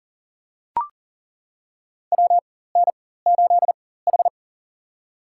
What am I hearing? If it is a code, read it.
Morse code, heard as a keyed tone, about 40 wpm, spaced as for 19 wpm, 700 Hz. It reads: WN8H
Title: USA Call Signs - 4 Characters - 4x Character Spacing - 40wpm